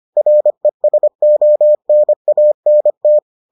Morse code, heard as a keyed tone, about 25 wpm, 600 Hz. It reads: RESONANT